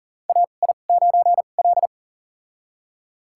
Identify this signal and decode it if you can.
Morse code, heard as a keyed tone, about 40 wpm, spaced as for 29 wpm, 700 Hz. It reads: AI9L
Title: USA Call Signs - 4 Characters - 2x Character Spacing - 40wpm